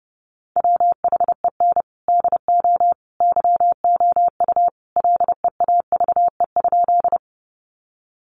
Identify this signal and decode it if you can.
Morse code, heard as a keyed tone, about 30 wpm, 700 Hz. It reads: WHEN DO YOU LEAVE?
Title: Sentences from Top 400 Words 30wpm